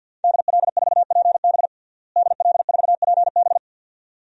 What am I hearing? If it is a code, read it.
Morse code, heard as a keyed tone, about 45 wpm, 700 Hz. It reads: DL4PB DL4PB
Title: DXCC-100 Call Signs - 5 Characters - Repeated 2x - 50wpm